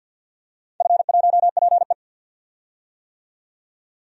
Morse code, heard as a keyed tone, about 47 wpm, 700 Hz. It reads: F1PE
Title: DXCC-100 Call Signs - 4 Characters - 50wpm